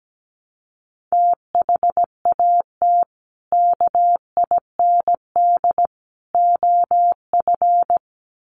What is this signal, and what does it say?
Morse code, heard as a keyed tone, about 17 wpm, 700 Hz. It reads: THAT KIND OF
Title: Sets of 3 Words 17wpm